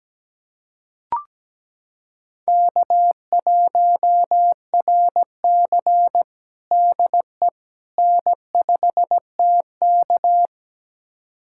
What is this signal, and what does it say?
Morse code, heard as a keyed tone, about 17 wpm, 700 Hz. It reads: K1RC DE N5TK
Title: USA Callsign DE Callsign - 4 Characters - 17wpm